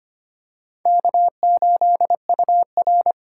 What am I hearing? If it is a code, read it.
Morse code, heard as a keyed tone, about 25 wpm, 700 Hz. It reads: K8UR
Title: USA Call Signs - 4x Character Spacing - 25wpm